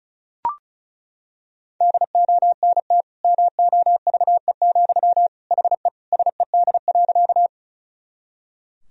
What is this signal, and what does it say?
Morse code, heard as a keyed tone, about 36 wpm, 700 Hz. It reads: DONT MOVE, HE SED.
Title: Sentences from Top 300 Words and Abbreviations 35wpm